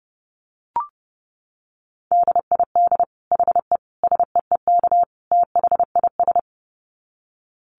Morse code, heard as a keyed tone, about 30 wpm, 700 Hz. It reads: DID HE SEEK THIS
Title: Dit Challenge 30wpm